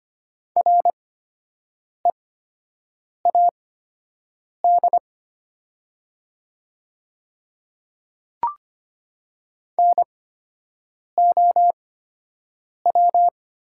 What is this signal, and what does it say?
Morse code, heard as a keyed tone, about 25 wpm, spaced as for 7 wpm, 700 Hz. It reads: READ NOW
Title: 8x Character Spacing - Top 200 Words - 25wpm